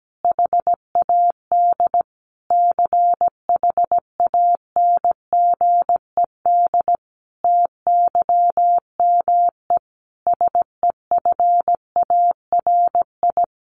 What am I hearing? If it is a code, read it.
Morse code, heard as a keyed tone, about 17 wpm, 700 Hz. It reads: HAD CHANGED TYME SEFARI
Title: A Sound Of Thunder 17wpm